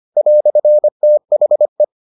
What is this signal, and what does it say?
Morse code, heard as a keyed tone, about 25 wpm, 600 Hz. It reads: "THE